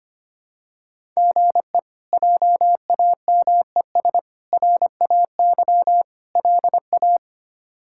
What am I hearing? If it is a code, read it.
Morse code, heard as a keyed tone, about 25 wpm, 700 Hz. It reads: GE JAMES RAY LA
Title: K1USN SST QSO Practice - 25wpm